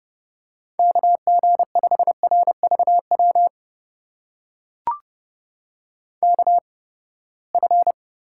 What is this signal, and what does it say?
Morse code, heard as a keyed tone, about 30 wpm, spaced as for 21 wpm, 700 Hz. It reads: KG5RVW KF
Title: USA Call Signs - 6 Characters - 8x Character Spacing - 30wpm